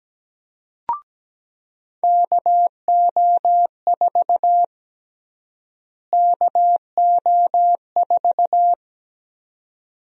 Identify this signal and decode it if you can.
Morse code, heard as a keyed tone, about 17 wpm, 700 Hz. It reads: KO4 KO4
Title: USA Call Sign Prefixes - Repeated 2x - 17wpm